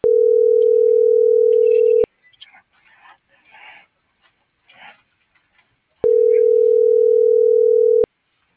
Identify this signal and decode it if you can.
Ringback tone (North America), 440 + 480 Hz.